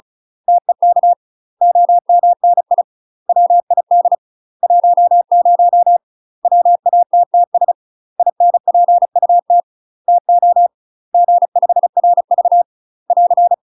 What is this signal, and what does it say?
Morse code, heard as a keyed tone, about 34 wpm, 700 Hz. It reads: TEK OMNI WID 10 WATTS INPUT TO G5RV <AR>
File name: UK MASTER QSOs TRAINING / 35WPM / UK MASTER QSOs 0026